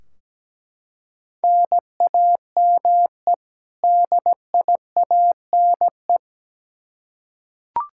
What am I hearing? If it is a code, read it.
Morse code, heard as a keyed tone, about 17 wpm, 700 Hz. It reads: NAME DIANE
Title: K1USN SST QSO Practice 17wpm